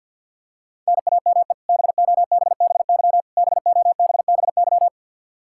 Morse code, heard as a keyed tone, about 52 wpm, 700 Hz. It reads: NAME BOBBY BOBBY
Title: Practice QSO - Exchange 1 - Beginning - 50wpm